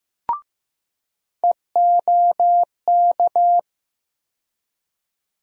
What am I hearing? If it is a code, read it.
Morse code, heard as a keyed tone, about 15 wpm, 700 Hz. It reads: EOK
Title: USA Call Sign Postfixes - 3 Letters 15wpm